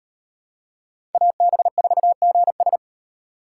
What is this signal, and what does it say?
Morse code, heard as a keyed tone, about 35 wpm, 700 Hz. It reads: AB4GS